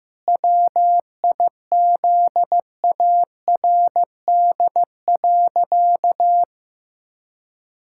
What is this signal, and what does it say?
Morse code, heard as a keyed tone, about 15 wpm, 700 Hz. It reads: WIZARD.
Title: Pangram Challenge v2 - 15wpm